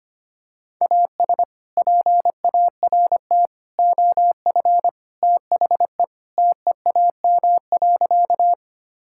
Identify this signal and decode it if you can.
Morse code, heard as a keyed tone, about 25 wpm, 700 Hz. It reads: AS PART OF THE TEAM.
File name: practice / 25wpm / Sentences with 3-Word Phrases - 25wpm